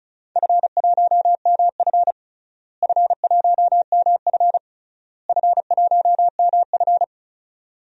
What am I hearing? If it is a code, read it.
Morse code, heard as a keyed tone, about 36 wpm, 700 Hz. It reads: F1MF F1MF F1MF
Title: DXCC-100 Call Signs - 4 Characters - Repeated 3x - 35wpm